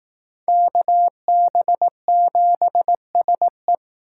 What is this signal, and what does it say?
Morse code, heard as a keyed tone, about 18 wpm, 700 Hz. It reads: KB7SE
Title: Speed Racing - USA 2x2 Call Signs 15wpm